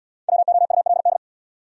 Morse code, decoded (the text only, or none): WOULD